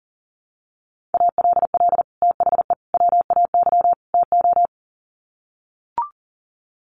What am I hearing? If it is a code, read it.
Morse code, heard as a keyed tone, about 40 wpm, 700 Hz. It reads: ALL THE WAY TO
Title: Sets of 4 Words 40wpm